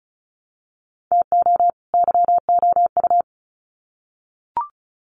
Morse code, heard as a keyed tone, about 34 wpm, 700 Hz. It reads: TO YOU
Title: Sets of 2 Words 35wpm